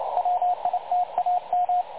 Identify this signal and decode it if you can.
Morse code, heard as a keyed tone, about 28 wpm, 720 Hz, with radio noise. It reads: WITAM
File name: dlazaawansowanych / nagraniazpasma / websdr recording 3565.0kHz